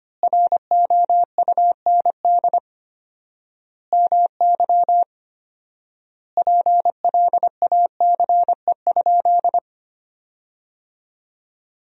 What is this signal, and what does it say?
Morse code, heard as a keyed tone, about 25 wpm, 700 Hz. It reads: ROUND MY PLACE?
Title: 4x Word-Spacing - Sentences from Top 200 Words - 25wpm